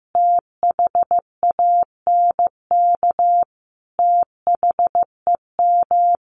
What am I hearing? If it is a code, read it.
Morse code, heard as a keyed tone, about 15 wpm, 700 Hz. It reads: THANK THEM
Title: Sentences from Top 1400 Words 15wpm